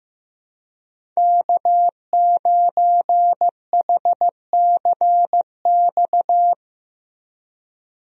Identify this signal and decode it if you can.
Morse code, heard as a keyed tone, about 15 wpm, 700 Hz. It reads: K9HCX
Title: USA Call Signs - 5 Characters - 15wpm